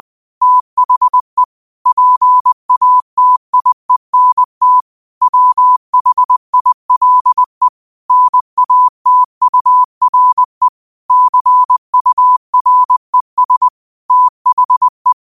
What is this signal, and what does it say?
Morse code, heard as a keyed tone, about 20 wpm, 1000 Hz. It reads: THE PATIENT WHILE NATURE CURES THE
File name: qotdfiles / 20wpm